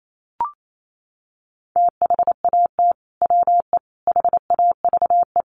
Morse code, heard as a keyed tone, about 28 wpm, 700 Hz. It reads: THAT WE HAVE
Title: Sets of 3 Words 28wpm